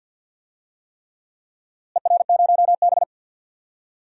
Morse code, heard as a keyed tone, about 47 wpm, 700 Hz. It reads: ER0B
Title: DXCC-100 Call Signs - 4 Characters - Repeated 3x - 50wpm